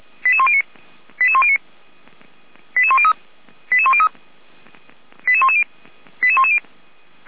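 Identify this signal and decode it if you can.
Five-tone selective call (ZVEI), heard as quick runs of five tones, tones of about 70 ms. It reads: 88109 88109 88103 88103 88110 88110
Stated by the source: ZVEI five-tone call received by SDR, from the feuerwehr-orlamuende.org (f85360) audio archive.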